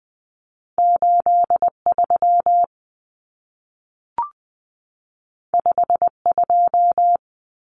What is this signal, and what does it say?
Morse code, heard as a keyed tone, about 20 wpm, 700 Hz. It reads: 83 52
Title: Common Cut Numbers - 2-Digit 20wpm